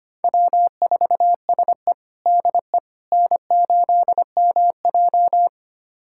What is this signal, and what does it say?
Morse code, heard as a keyed tone, about 25 wpm, 700 Hz. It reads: W4SE DE N8MJ
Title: USA Callsign DE Callsign - 4 Characters - 25wpm